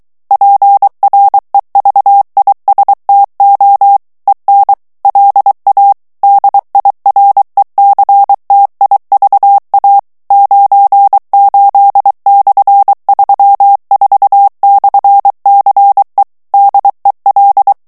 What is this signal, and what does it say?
Morse code, heard as a keyed tone, about 23 wpm, 800 Hz. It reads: PREVISTO EN LA DIRECTIVA 98/34/CE DEL
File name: morse mp3 / ESPAGNOL TEXTE EN CLAIR 2 VITESSE 22 MOTS MINUTES